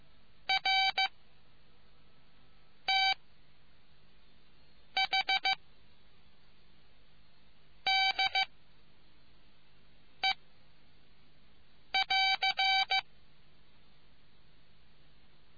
Morse code, heard as a keyed tone, about 15 wpm, 770 Hz. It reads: R T H D E <AR>